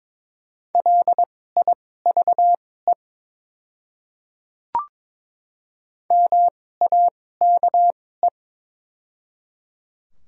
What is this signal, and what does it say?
Morse code, heard as a keyed tone, about 22 wpm, spaced as for 16 wpm, 700 Hz. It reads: LIVE MAKE
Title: 2x Character Spacing - Top 300 Words - 22wpm